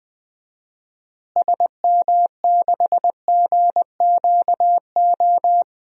Morse code, heard as a keyed tone, about 20 wpm, 700 Hz. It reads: SM6GQO
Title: DXCC-100 Call Signs - 6 Characters - Repeated 2x - 20wpm